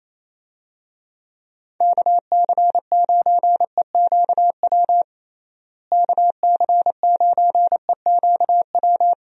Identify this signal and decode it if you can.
Morse code, heard as a keyed tone, about 27 wpm, 700 Hz. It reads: KC9EQW KC9EQW
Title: USA Call Signs - 6 Characters - Repeated 2x - 28wpm